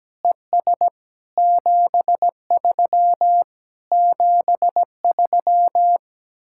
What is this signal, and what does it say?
Morse code, heard as a keyed tone, about 17 wpm, 700 Hz. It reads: ES 73 73
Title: Speed-Racing Common QSO Phrases Encore - 17wpm